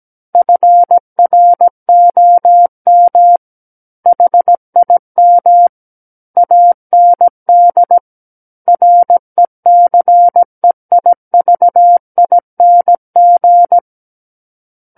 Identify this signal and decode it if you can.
Morse code, heard as a keyed tone, about 17 wpm, 700 Hz. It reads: FROM HIM AND RECEIVING
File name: Book One / 08-Chapter 8 - 17 WPM